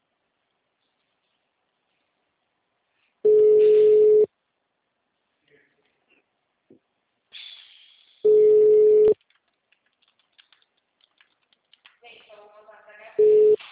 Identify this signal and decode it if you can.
Ringback tone (Europe, 425 Hz), with speech.